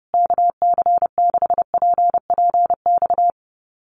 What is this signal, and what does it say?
Morse code, heard as a keyed tone, about 28 wpm, 700 Hz. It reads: KC6PPX
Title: USA Call Signs Repeated 3x 30wpm